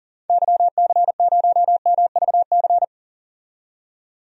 Morse code, heard as a keyed tone, about 37 wpm, 700 Hz. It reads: YC0MVC
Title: DXCC-100 Call Signs - 6 Characters - 40wpm